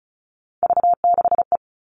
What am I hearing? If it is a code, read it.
Morse code, heard as a keyed tone, about 34 wpm, 700 Hz. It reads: V6E
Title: All Cut Numbers - 3-Digit 35wpm